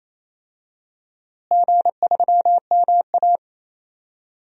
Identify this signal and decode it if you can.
Morse code, heard as a keyed tone, about 28 wpm, 700 Hz. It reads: G3MA
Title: DXCC-100 Call Signs - 4 Characters - Repeated 3x - 28wpm